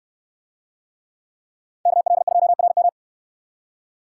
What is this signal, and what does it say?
Morse code, heard as a keyed tone, about 69 wpm, 700 Hz. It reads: KL2RM